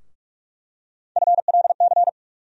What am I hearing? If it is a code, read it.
Morse code, heard as a keyed tone, about 46 wpm, 700 Hz. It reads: FLC
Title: USA Call Sign Postfixes - 3 Letters 45wpm